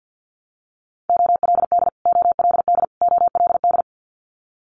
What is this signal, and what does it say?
Morse code, heard as a keyed tone, about 50 wpm, 700 Hz. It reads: OLD OLD OLD
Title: Top 300 Words Repeated 50wpm